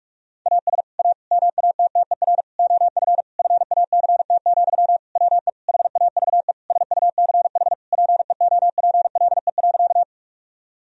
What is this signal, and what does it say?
Morse code, heard as a keyed tone, about 41 wpm, 700 Hz. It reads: AS A MATTER OF FACT, WE HAVE SUCH PEOPLE.
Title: Sentences from Top 1000 Words - Review Part 3 - 45wpm